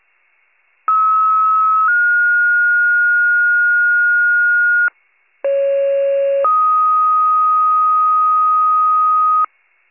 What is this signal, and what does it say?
Two-tone page, heard as a one-second tone, then a three-second tone, about 1290 then 1430 Hz, about 550 then 1250 Hz. It is heard from 0.9 to 9.5 s.